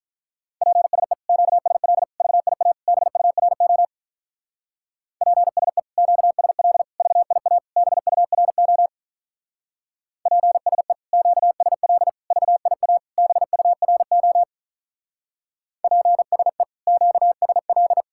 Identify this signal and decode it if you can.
Morse code, heard as a keyed tone, about 46 wpm, 700 Hz. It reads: PSE QSL VIA BURO PSE QSL VIA BURO PSE QSL VIA BURO PSE QSL